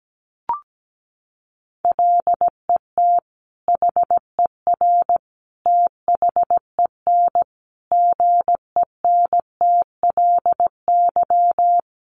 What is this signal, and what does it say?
Morse code, heard as a keyed tone, about 17 wpm, 700 Hz. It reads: LET HER THEN GENTLY
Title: Sentences from Top 1500 Words 17wpm